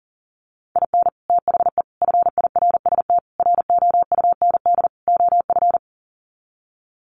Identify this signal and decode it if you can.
Morse code, heard as a keyed tone, about 40 wpm, 700 Hz. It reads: IN THE FIRST ROUND OF
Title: Sets of 5 Words Encore 40wpm